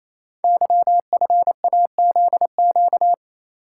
Morse code, heard as a keyed tone, about 29 wpm, 700 Hz. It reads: YFAZQ
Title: Code Groups - 5 Letter:Numbers - 28wpm